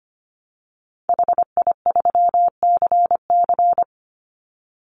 Morse code, heard as a keyed tone, about 25 wpm, 700 Hz. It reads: HI3CC
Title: Call Signs Repeated 3x - 25wpm